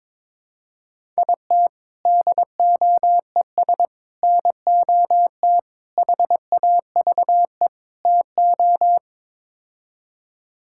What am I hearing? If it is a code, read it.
Morse code, heard as a keyed tone, about 22 wpm, 700 Hz. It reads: IT DOES NOT HAVE TO
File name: practice / 22wpm / Build A Phrase - 5-Word Phrases - 22wpm